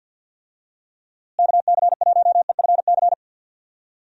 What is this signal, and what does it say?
Morse code, heard as a keyed tone, about 52 wpm, 700 Hz. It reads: KC1EFC
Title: USA Call Signs - 6 Characters - Repeated 3x - 50wpm